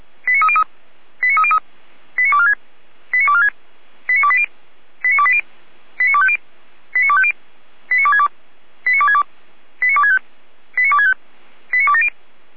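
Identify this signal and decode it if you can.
Five-tone selective call (ZVEI), heard as quick runs of five tones, tones of about 70 ms. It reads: 89303 89303 89246 89246 89280 89280 89250 89250 89262 89262 89265 89265 89270
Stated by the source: Five-tone ZVEI selective call 89235, received on SDR, from the feuerwehr-orlamuende.org audio archive.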